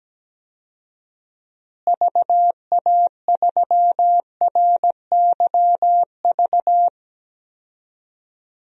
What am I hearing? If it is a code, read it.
Morse code, heard as a keyed tone, about 17 wpm, 700 Hz. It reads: VA3RYV
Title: DXCC-100 Call Signs - 6 Characters - Repeated 2x - 17wpm